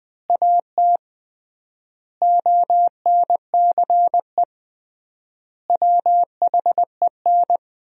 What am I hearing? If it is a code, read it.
Morse code, heard as a keyed tone, about 20 wpm, 700 Hz. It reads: AT ONCE WHEN
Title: 3x Word-Spacing - Sentences from Top 300 Words - 20wpm